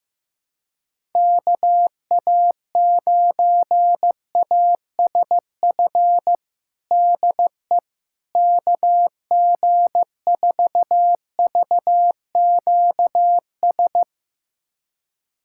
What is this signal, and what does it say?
Morse code, heard as a keyed tone, about 16 wpm, 700 Hz. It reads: KA9ASF DE KG4VQS
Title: USA Callsign DE Callsign 15wpm